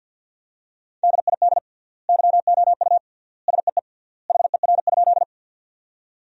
Morse code, heard as a keyed tone, about 50 wpm, 700 Hz. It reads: DID YOU SEE HER?
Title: 3x Word-Spacing - Sentences from Top 100 Words - 50wpm